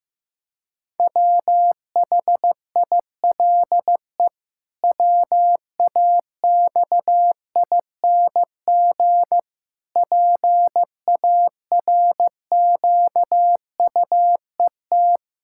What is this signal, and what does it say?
Morse code, heard as a keyed tone, about 15 wpm, 700 Hz. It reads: WHILE WAXING PARQUET
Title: Pangram Challenge v2 - 15wpm